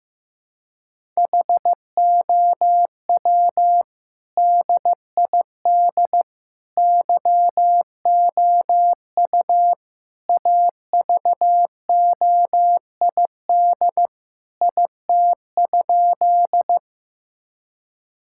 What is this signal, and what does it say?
Morse code, heard as a keyed tone, about 15 wpm, 700 Hz. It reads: HOW DID YOU AVOID IT?
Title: Sentences from Top 1000 Words - Review Part 6 - 15wpm